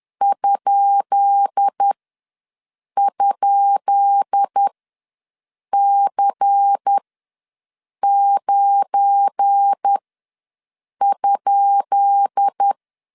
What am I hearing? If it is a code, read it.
Morse code, heard as a keyed tone, about 10 wpm, spaced as for 6 wpm, 800 Hz. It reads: ??C9?
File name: LEZIONI / 10 SIGNS / 10B INT ws 10wpm 4134